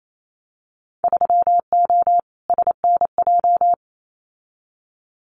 Morse code, heard as a keyed tone, about 28 wpm, 700 Hz. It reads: 3O SNJ
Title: Winter Field Day Exchanges Encore 28wpm